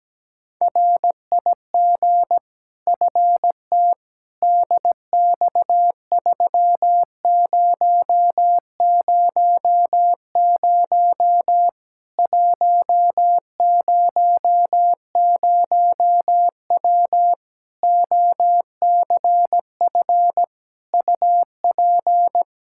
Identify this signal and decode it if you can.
Morse code, heard as a keyed tone, about 17 wpm, 700 Hz. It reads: RIG FT DX3000 100W OCF UP
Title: K1USN SST QSO Practice 17wpm